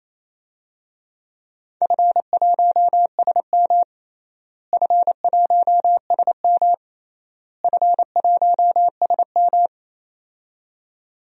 Morse code, heard as a keyed tone, about 28 wpm, 700 Hz. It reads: F1SM F1SM F1SM
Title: DXCC-100 Call Signs - 4 Characters - Repeated 3x - 28wpm